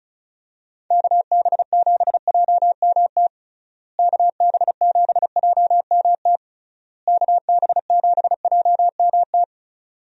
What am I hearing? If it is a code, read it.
Morse code, heard as a keyed tone, about 36 wpm, 700 Hz. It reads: KB7JMT KB7JMT KB7JMT
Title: USA Call Signs - 6 Characters - Repeated 3x - 35wpm